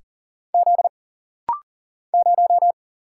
Morse code, heard as a keyed tone, about 40 wpm, 700 Hz. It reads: Z 0
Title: Single Letter-Number Alt - Rapid-Fire 40wpm